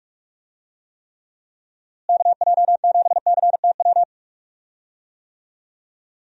Morse code, heard as a keyed tone, about 46 wpm, 700 Hz. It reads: KJ7CTW
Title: USA Call Signs - 6 Characters - Repeated 3x - 45wpm